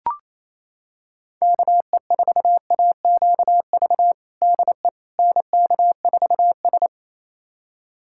Morse code, heard as a keyed tone, about 28 wpm, 700 Hz. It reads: KE4AQV DE NK4S
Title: USA Callsign DE Callsign 28wpm